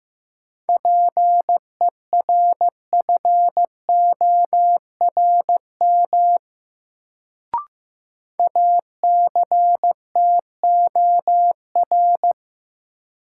Morse code, heard as a keyed tone, about 15 wpm, 700 Hz. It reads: PERFORM ACTOR
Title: Sentences from Top 1500 Words - New Word Drill Alt - 15wpm